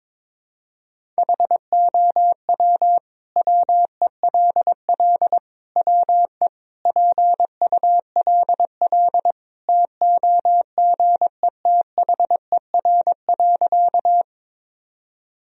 Morse code, heard as a keyed tone, about 21 wpm, 700 Hz. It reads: HOW WELL WE PULL TOGETHER.
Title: Sentences from Top 1000 Words - Review Part 2 - 22wpm